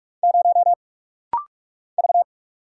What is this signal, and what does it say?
Morse code, heard as a keyed tone, about 44 wpm, 700 Hz. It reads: 0 V
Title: Single Letter-Number Alt - Rapid-Fire 45wpm